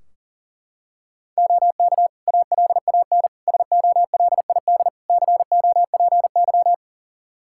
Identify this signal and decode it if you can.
Morse code, heard as a keyed tone, about 40 wpm, 700 Hz. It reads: OK ALAN SOLID COPY
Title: Practice QSO - Exchange 3 - More Info - 40wpm